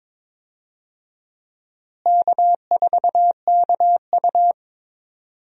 Morse code, heard as a keyed tone, about 22 wpm, 700 Hz. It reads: K4KU
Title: USA Call Signs - 4 Characters - Repeated 3x - 22wpm